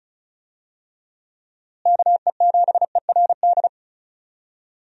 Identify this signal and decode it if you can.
Morse code, heard as a keyed tone, about 36 wpm, 700 Hz. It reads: KE7ERD
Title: USA Call Signs - 6 Characters - Repeated 2x - 35wpm